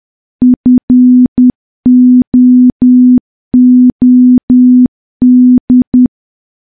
Morse code, heard as a keyed tone, about 10 wpm, 250 Hz. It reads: FOOD
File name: morse / morse-training-10WPM